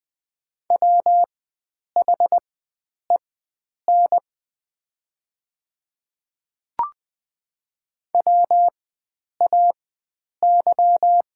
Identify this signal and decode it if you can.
Morse code, heard as a keyed tone, about 20 wpm, spaced as for 9 wpm, 700 Hz. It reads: WHEN WAY
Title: 4x Character Spacing - Top 100 Words - 20wpm